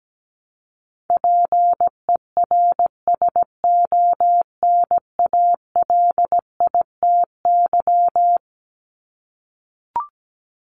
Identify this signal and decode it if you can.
Morse code, heard as a keyed tone, about 17 wpm, 700 Hz. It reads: PERSONALITY
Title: Top 1251-1500 Words 17wpm